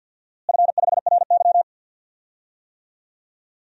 Morse code, heard as a keyed tone, about 51 wpm, 700 Hz. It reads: F5RY
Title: DXCC-100 Call Signs - 4 Characters - Repeated 2x - 50wpm